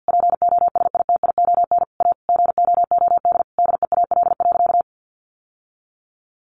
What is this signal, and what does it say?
Morse code, heard as a keyed tone, about 50 wpm, 700 Hz. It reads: POSITION A GOOD DEAL.
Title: Sentences from Top 1200 Words 50wpm